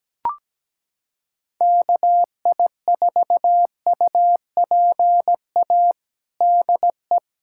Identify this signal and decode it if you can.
Morse code, heard as a keyed tone, about 17 wpm, 700 Hz. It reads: KI4UPA DE
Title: USA Callsign DE Callsign 17wpm